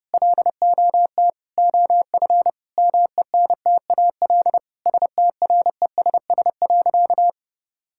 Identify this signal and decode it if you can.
Morse code, heard as a keyed tone, about 30 wpm, 700 Hz. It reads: LOT OF MENTAL STRESS.
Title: Sentences from Top 1000 Words - Review Part 6 - 30wpm